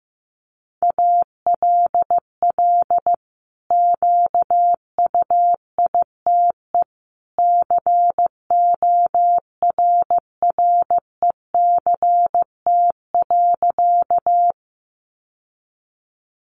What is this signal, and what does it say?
Morse code, heard as a keyed tone, about 15 wpm, 700 Hz. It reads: ALL QUITE CORRECT.
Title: Sentences from Top 700 Words 15wpm